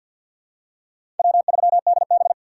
Morse code, heard as a keyed tone, about 51 wpm, 700 Hz. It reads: W3DB
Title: USA Call Signs - 3x Character Spacing - 50wpm